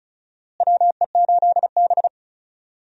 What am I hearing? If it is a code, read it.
Morse code, heard as a keyed tone, about 36 wpm, 700 Hz. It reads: WE8B
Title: USA Call Signs - 4 Characters - Repeated 2x - 35wpm